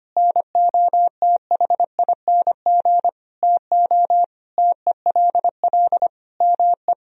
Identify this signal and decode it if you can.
Morse code, heard as a keyed tone, about 25 wpm, 700 Hz. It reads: NOTHING TO TELL ME
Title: Sentences from Top 1000 Words - Review Part 4 - 25wpm